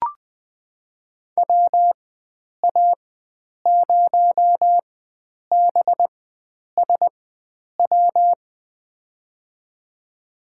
Morse code, heard as a keyed tone, about 20 wpm, spaced as for 9 wpm, 700 Hz. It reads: WA0BSW
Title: USA Call Signs - 6 Characters - 4x Character Spacing - 20wpm